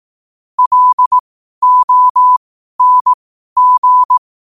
Morse code, heard as a keyed tone, about 18 wpm, spaced as for 12 wpm, 1000 Hz. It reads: LONG
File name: qotdfiles / 12wpm